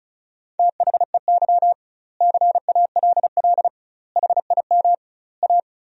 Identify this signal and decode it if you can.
Morse code, heard as a keyed tone, about 34 wpm, 700 Hz. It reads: THEY CALL HIM A
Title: 2x Word-Spacing - Sentences from Top 500 Words - 35wpm